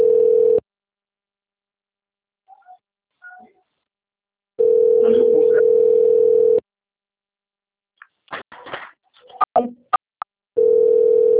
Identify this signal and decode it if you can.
Ringback tone (North America), 440 + 480 Hz.